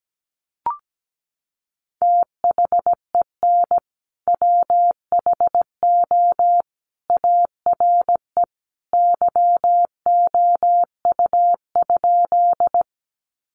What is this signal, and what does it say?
Morse code, heard as a keyed tone, about 17 wpm, 700 Hz. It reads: THEN WHO ARE YOU?
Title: Sentences from Top 100 Words 17wpm